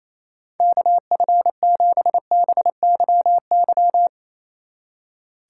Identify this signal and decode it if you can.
Morse code, heard as a keyed tone, about 36 wpm, 700 Hz. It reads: KF7BYY